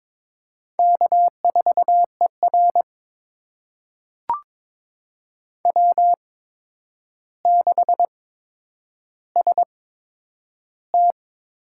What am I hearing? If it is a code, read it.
Morse code, heard as a keyed tone, about 21 wpm, spaced as for 9 wpm, 700 Hz. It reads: K4ER W6ST
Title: USA Call Signs - 4 Characters - 8x Character Spacing - 22wpm